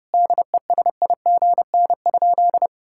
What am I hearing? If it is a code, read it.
Morse code, heard as a keyed tone, about 30 wpm, 700 Hz. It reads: DESIGN?
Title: Sentences from Top 1000 Words - Review Part 5 - 30wpm